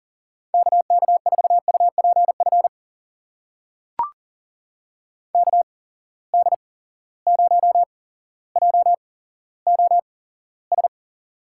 Morse code, heard as a keyed tone, about 40 wpm, spaced as for 17 wpm, 700 Hz. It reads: KK4UPF KD0JOS
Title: USA Call Signs - 6 Characters - 8x Character Spacing - 40wpm